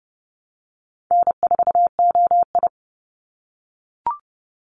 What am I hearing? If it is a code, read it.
Morse code, heard as a keyed tone, about 30 wpm, 700 Hz. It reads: N4OI
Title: Call Signs - 30wpm